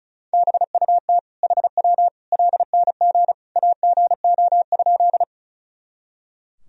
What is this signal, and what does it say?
Morse code, heard as a keyed tone, about 36 wpm, 700 Hz. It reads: BUT HW LNG AGO?
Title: Sentences from Top 500 Words and Abbreviations 35wpm